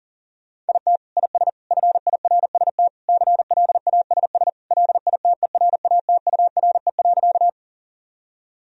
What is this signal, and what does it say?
Morse code, heard as a keyed tone, about 37 wpm, 700 Hz. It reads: IT IS FIRST CLASS LITERATURE.
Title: Sentences from Top 1000 Words - Review Part 3 - 40wpm